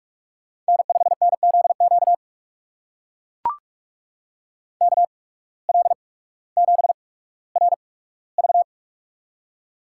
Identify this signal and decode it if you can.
Morse code, heard as a keyed tone, about 44 wpm, spaced as for 19 wpm, 700 Hz. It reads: N5NZQ KL7RV
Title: USA Call Signs - 5 Characters - 8x Character Spacing - 45wpm